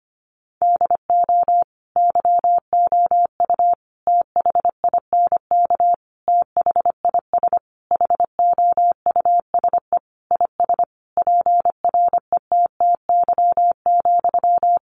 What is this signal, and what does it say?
Morse code, heard as a keyed tone, about 25 wpm, 700 Hz. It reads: DO YOU THINK THIS HOUSE IS PRETTY,
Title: Sentences from Top 1000 Words 25wpm